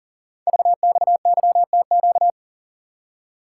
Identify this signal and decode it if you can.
Morse code, heard as a keyed tone, about 38 wpm, 690 Hz. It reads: VXYTQ
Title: Code Groups - 5 Letter:Numbers - 40wpm